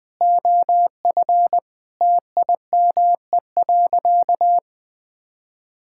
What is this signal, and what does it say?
Morse code, heard as a keyed tone, about 20 wpm, 700 Hz. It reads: OF TIME.
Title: Sentences with 3-Word Phrases - Spoken Beginning - 20wpm